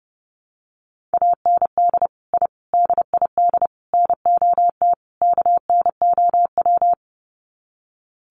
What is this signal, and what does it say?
Morse code, heard as a keyed tone, about 30 wpm, 700 Hz. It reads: AND I DID NOT KNOW
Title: Sets of 5 Words Encore 30wpm